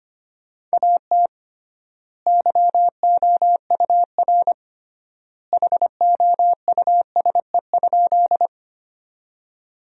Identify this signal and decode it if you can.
Morse code, heard as a keyed tone, about 25 wpm, 700 Hz. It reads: AT YOUR HOUSE?
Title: 3x Word-Spacing - Sentences from Top 200 Words - 25wpm